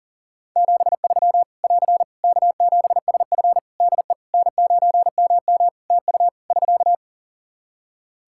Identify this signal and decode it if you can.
Morse code, heard as a keyed tone, about 40 wpm, 700 Hz. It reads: 73 <AR> K7SF DE N9MM TU <SK>
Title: K1USN SST QSO Practice 40wpm